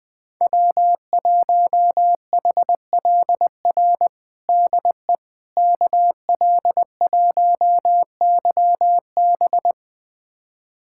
Morse code, heard as a keyed tone, about 20 wpm, 700 Hz. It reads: W1HLR DE KL1YB
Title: USA Callsign DE Callsign - 5 Characters - 20wpm